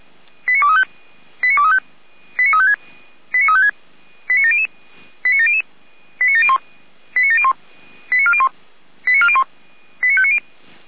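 Five-tone selective call (ZVEI), heard as quick runs of five tones, tones of about 70 ms. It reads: 89235 89235 89356 89356 89700 89700 89701 89701 89401 89401 89590
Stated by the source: ZVEI five-tone call received on SDR.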